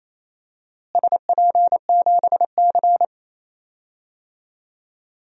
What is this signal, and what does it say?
Morse code, heard as a keyed tone, about 28 wpm, 700 Hz. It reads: SP7C
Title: DXCC-100 Call Signs - 4 Characters - Repeated 3x - 28wpm